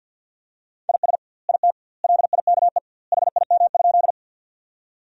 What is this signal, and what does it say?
Morse code, heard as a keyed tone, about 50 wpm, 700 Hz. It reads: IS IT LIKE HIM?
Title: 2x Word-Spacing - Sentences from Top 100 Words - 50wpm